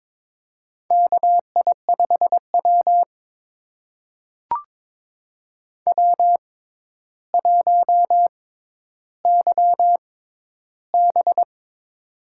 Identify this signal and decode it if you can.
Morse code, heard as a keyed tone, about 22 wpm, spaced as for 11 wpm, 700 Hz. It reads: KI5W W1YB
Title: USA Call Signs - 6x Character Spacing - 22wpm